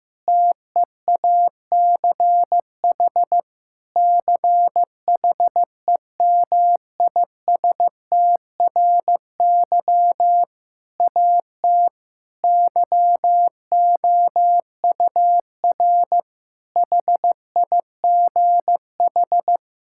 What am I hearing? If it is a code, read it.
Morse code, heard as a keyed tone, about 16 wpm, 700 Hz. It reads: TEACH CHEMISTRY AT YOUR HIGH
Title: Sentences from Top 1000 Words - Review Part 6 - 15wpm